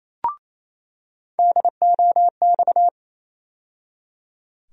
Morse code, heard as a keyed tone, about 28 wpm, 700 Hz. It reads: DOX
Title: USA Call Sign Postfixes - 3 Letters 28wpm